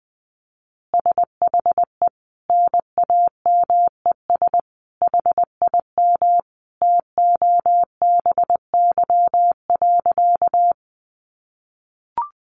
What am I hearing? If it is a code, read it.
Morse code, heard as a keyed tone, about 20 wpm, 700 Hz. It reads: SHE NAMES HIM TOBY.
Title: Easy Phrases 20wpm